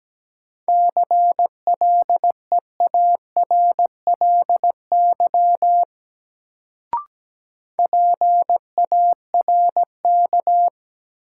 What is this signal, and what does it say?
Morse code, heard as a keyed tone, about 17 wpm, 700 Hz. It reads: CLEARLY PARK
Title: Sentences from Top 1300 Words - New Word Drill - 17wpm